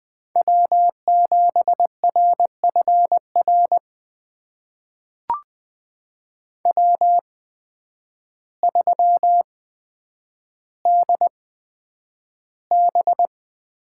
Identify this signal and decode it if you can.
Morse code, heard as a keyed tone, about 20 wpm, spaced as for 9 wpm, 700 Hz. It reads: W7RFR W3DB
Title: USA Call Signs - 8x Character Spacing - 20wpm